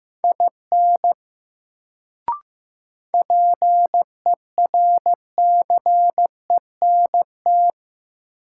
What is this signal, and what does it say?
Morse code, heard as a keyed tone, about 16 wpm, 700 Hz. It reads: IN PERCENT
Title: Sets of 2 Words - New Word Drill - 15wpm